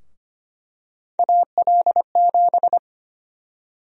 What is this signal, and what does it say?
Morse code, heard as a keyed tone, about 25 wpm, 700 Hz. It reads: AL7
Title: USA Call Sign Prefixes - 25wpm